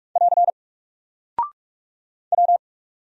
Morse code, heard as a keyed tone, about 46 wpm, 700 Hz. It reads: <AR> W
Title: Single Character (Letter:Number:Punctuation:Prosign) Alt 45wpm